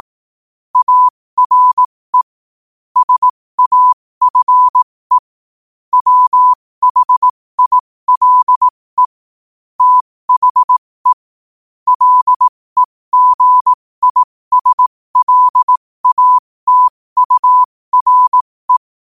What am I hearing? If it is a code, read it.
Morse code, heard as a keyed tone, about 18 wpm, spaced as for 15 wpm, 1000 Hz. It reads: ARE SAFE WHILE THE LEGISLATURE
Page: 15 WPM morse code quotes